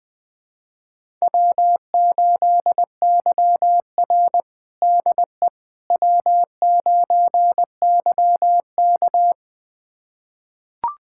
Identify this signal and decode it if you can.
Morse code, heard as a keyed tone, about 21 wpm, 700 Hz. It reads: W8YR DE W9YK
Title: USA Callsign DE Callsign - 4 Characters - 20wpm